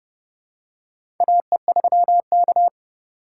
Morse code, heard as a keyed tone, about 30 wpm, 700 Hz. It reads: AE3K
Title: USA Call Signs - 4 Characters - 3x Character Spacing - 30wpm